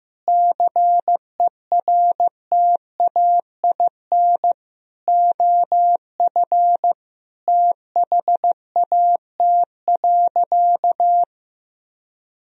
Morse code, heard as a keyed tone, about 15 wpm, 700 Hz. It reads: CERTAIN OF THAT.
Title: Sentences from Top 1000 Words - Review Part 2 - 15wpm